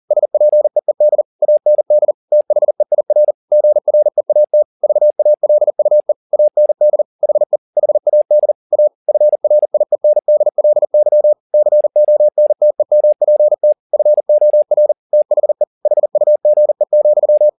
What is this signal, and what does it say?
Morse code, heard as a keyed tone, about 40 wpm, 600 Hz. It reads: SPEED AND THEIR GREAT VALUE AND HE HAD A FRIENDLY CONTEMPT FOR THE HUGE,